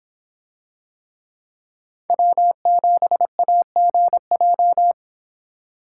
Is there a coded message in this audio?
Yes: Morse code from 2.1 to 4.9 s, heard as a keyed tone, about 26 wpm, 700 Hz. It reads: W7AGJ